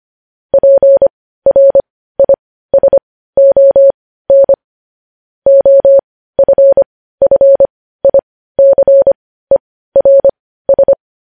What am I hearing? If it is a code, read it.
Morse code, heard as a keyed tone, about 25 wpm, spaced as for 15 wpm, 550 Hz. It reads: PRISON OFFICERS
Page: News Headlines in Morse Code at 15 WPM